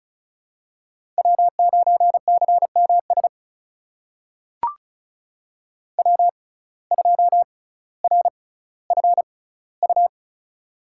Morse code, heard as a keyed tone, about 36 wpm, spaced as for 18 wpm, 700 Hz. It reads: W9CMS W2RFU
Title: USA Call Signs - 5 Characters - 6x Character Spacing - 35wpm